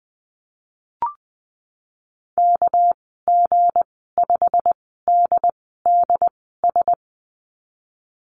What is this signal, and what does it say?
Morse code, heard as a keyed tone, about 20 wpm, spaced as for 15 wpm, 700 Hz. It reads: KG5DDS
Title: 238 Lesson 38g - Letter q - Call Signs 20wpm